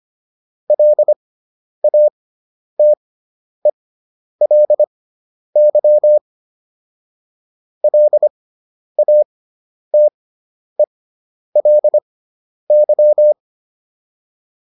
Morse code, heard as a keyed tone, about 25 wpm, spaced as for 10 wpm, 600 Hz. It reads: LATELY LATELY